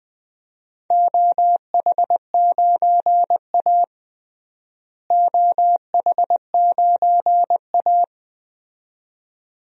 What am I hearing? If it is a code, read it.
Morse code, heard as a keyed tone, about 21 wpm, 700 Hz. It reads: OH9A OH9A
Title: DXCC-100 Call Signs - 4 Characters - Repeated 2x - 20wpm